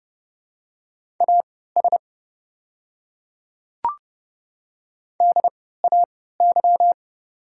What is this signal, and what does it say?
Morse code, heard as a keyed tone, about 30 wpm, spaced as for 17 wpm, 700 Hz. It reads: AS DAY
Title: 3x Character Spacing - Top 200 Words - 30wpm